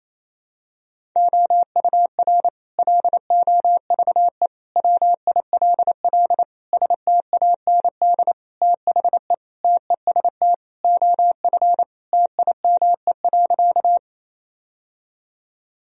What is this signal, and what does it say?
Morse code, heard as a keyed tone, about 28 wpm, 700 Hz. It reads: OUR LOVE WILL STAND THE TEST OF TIME.
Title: Sentences from Top 1000 Words - Review Part 3 - 28wpm